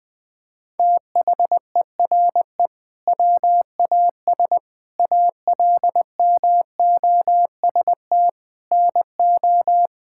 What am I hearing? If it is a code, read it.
Morse code, heard as a keyed tone, about 21 wpm, 700 Hz. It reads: THERE WAS ALMOST NO
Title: Sentences from Top 1000 Words - Review Part 6 - 20wpm